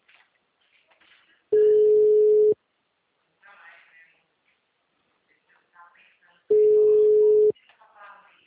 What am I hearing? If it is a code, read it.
Ringback tone (Europe, 425 Hz), with speech.
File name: out- -2033- - - .46255